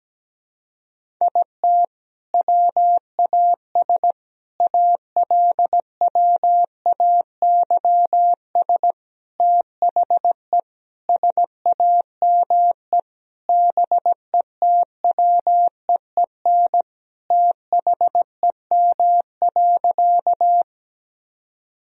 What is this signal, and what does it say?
Morse code, heard as a keyed tone, about 17 wpm, 700 Hz. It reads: IT WAS ALWAYS THE SAME BETWEEN THEM.
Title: Sentences from Top 1000 Words - Review Part 1 - 17wpm